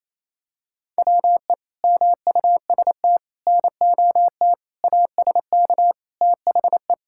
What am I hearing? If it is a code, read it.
Morse code, heard as a keyed tone, about 28 wpm, 700 Hz. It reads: WE MUST NOT ASK THE
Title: Sentences from Top 1600 Words 28wpm